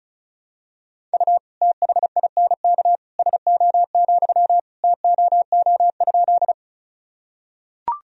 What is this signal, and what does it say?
Morse code, heard as a keyed tone, about 36 wpm, 700 Hz. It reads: U THINK SO, TOO?
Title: Sentences from Top 200 Words and Abbreviations 35wpm